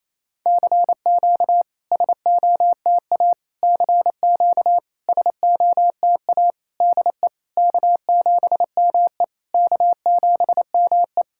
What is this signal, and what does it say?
Morse code, heard as a keyed tone, about 28 wpm, 690 Hz. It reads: CQ SOTA CQ SOTA DE K7ME K7ME
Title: SOTA QSO Practice 28wpm